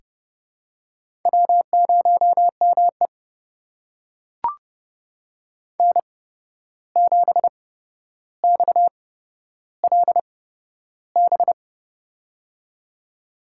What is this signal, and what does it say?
Morse code, heard as a keyed tone, about 32 wpm, spaced as for 12 wpm, 700 Hz. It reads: W0ME N7XLB
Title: USA Call Signs - 8x Character Spacing - 30wpm